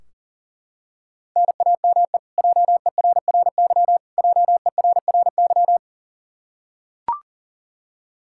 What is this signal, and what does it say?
Morse code, heard as a keyed tone, about 40 wpm, 700 Hz. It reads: NAME JERRY JERRY
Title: Practice QSO - Exchange 1 - Beginning - 40wpm